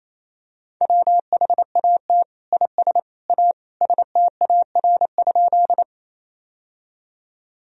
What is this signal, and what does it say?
Morse code, heard as a keyed tone, about 28 wpm, 700 Hz. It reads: WHAT IS A STAR?
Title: Sentences from Top 1000 Words - Review Part 2 - 28wpm